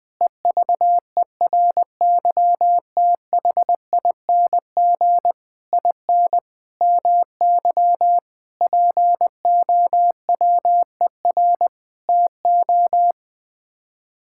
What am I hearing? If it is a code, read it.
Morse code, heard as a keyed tone, about 20 wpm, 700 Hz. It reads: EVERYTHING IN MY POWER TO
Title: Sets of 8 Words - 20wpm